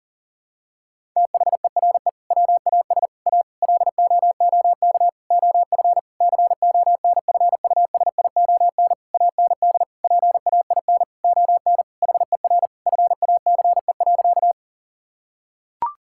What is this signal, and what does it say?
Morse code, heard as a keyed tone, about 40 wpm, 700 Hz. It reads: THERE WAS A LOOK OF CONFUSION AND PAIN ON HER FACE.
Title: Sentences from Top 1600 Words 40wpm